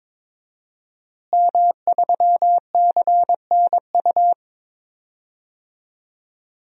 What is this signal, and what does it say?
Morse code, heard as a keyed tone, about 22 wpm, 700 Hz. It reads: M3CNU
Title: DXCC-100 Call Signs - 5 Characters - Repeated 3x - 22wpm